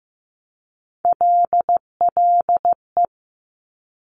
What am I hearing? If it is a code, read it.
Morse code, heard as a keyed tone, about 15 wpm, 700 Hz. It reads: LLE
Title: Top Three Letter Combinations in Words 15wpm